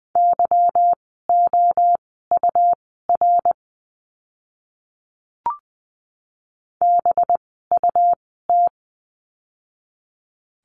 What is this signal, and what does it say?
Morse code, heard as a keyed tone, about 20 wpm, spaced as for 15 wpm, 700 Hz. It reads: YOUR BUT
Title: 185 Lesson 30f - Letter b - Words 20wpm